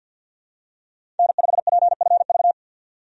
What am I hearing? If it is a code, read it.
Morse code, heard as a keyed tone, about 52 wpm, 700 Hz. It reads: N5PFV
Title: USA Call Signs - 5 Characters - 50wpm